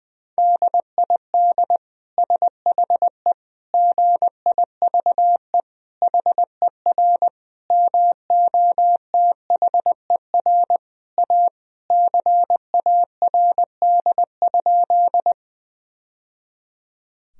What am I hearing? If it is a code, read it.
Morse code, heard as a keyed tone, about 21 wpm, 700 Hz. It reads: DID SHE GIVE HER MOTHER A CARD?
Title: Sentences from Top 1000 Words - Review Part 4 - 20wpm